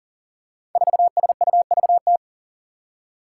Morse code, heard as a keyed tone, about 37 wpm, 700 Hz. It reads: 4SUVT
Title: Five Challenging Letter-Number Combinations 40wpm